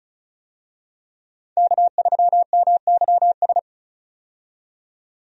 Morse code, heard as a keyed tone, about 34 wpm, 700 Hz. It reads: K3MYS
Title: USA Call Signs - 5 Characters - 35wpm